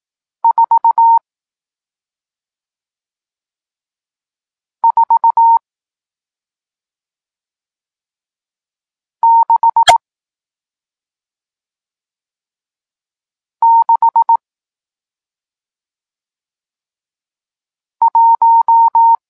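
Morse code, heard as a keyed tone, about 18 wpm, spaced as for 2 wpm, 920 Hz. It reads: 44661
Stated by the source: Morse code lesson teaching digits 1, 4 and 6, from SP9MOA, Niepołomicki Klub Krótkofalowców (Poland) amateur radio club.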